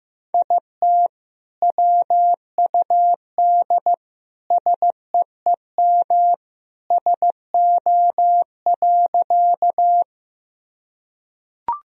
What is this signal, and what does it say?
Morse code, heard as a keyed tone, about 15 wpm, 700 Hz. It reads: IT WUD SEEM SO.
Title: Sentences from Top 300 Words and Abbreviations 15wpm